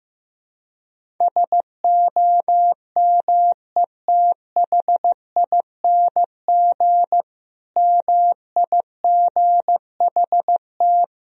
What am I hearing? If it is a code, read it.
Morse code, heard as a keyed tone, about 15 wpm, 700 Hz. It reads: SOMETHING MIGHT
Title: Sentences from Top 1000 Words - Review Part 1 - 15wpm